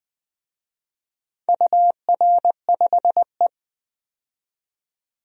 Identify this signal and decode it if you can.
Morse code, heard as a keyed tone, about 20 wpm, 700 Hz. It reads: UR5E
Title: DXCC-100 Call Signs - 4 Characters - 20wpm